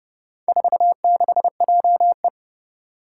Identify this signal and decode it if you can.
Morse code, heard as a keyed tone, about 30 wpm, 700 Hz. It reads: 46JE
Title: Four Challenging Letter-Number Combinations 30wpm